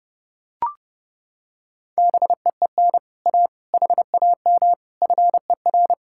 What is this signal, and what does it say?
Morse code, heard as a keyed tone, about 30 wpm, 700 Hz. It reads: BEEN A HAM FER